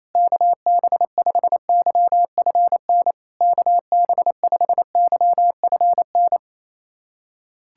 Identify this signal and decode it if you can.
Morse code, heard as a keyed tone, about 28 wpm, 690 Hz. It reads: KB5YFN KB5YFN
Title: SOTA QSO Practice 28wpm